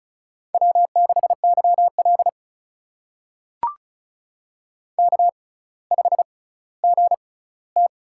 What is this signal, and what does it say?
Morse code, heard as a keyed tone, about 36 wpm, spaced as for 18 wpm, 700 Hz. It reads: W6YL K5GT
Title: USA Call Signs - 4 Characters - 6x Character Spacing - 35wpm